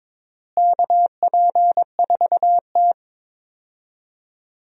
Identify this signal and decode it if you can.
Morse code, heard as a keyed tone, about 21 wpm, 700 Hz. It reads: KP4T
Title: DXCC-100 Call Signs - 4 Characters - Repeated 3x - 22wpm